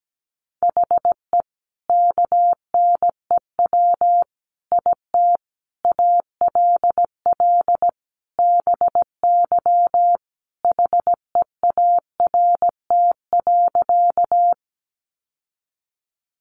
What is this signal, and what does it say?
Morse code, heard as a keyed tone, about 17 wpm, 700 Hz. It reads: HE KNEW IT ALL BY HEART.
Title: Sentences from Top 1200 Words 17wpm